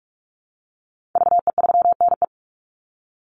Morse code, heard as a keyed tone, about 44 wpm, 700 Hz. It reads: VE3NE
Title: Call Signs Repeated 2x - 45wpm